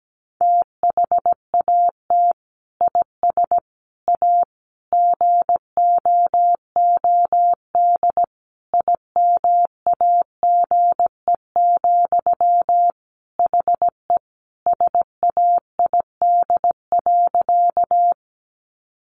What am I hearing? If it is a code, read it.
Morse code, heard as a keyed tone, about 17 wpm, 700 Hz. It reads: THAT IS A GOOD IMAGE, HE SAID.
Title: Sentences from Top 700 Words 17wpm